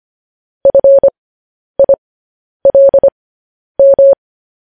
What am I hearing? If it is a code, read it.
Morse code, heard as a keyed tone, about 25 wpm, spaced as for 10 wpm, 550 Hz. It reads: FILM
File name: podcast / News headlines-1x-25x10